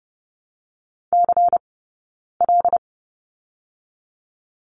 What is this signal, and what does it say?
Morse code, heard as a keyed tone, about 30 wpm, 700 Hz. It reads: C L
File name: Lesson 16d - Letter c - Two Characters 30wpm